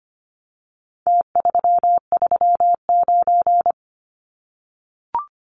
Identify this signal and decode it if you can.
Morse code, heard as a keyed tone, about 25 wpm, 700 Hz. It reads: T339
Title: Numbers - Contest Style - 4-Digit 25wpm